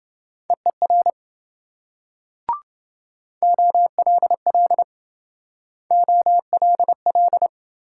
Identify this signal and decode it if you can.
Morse code, heard as a keyed tone, about 26 wpm, 700 Hz. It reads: EER OLL OLL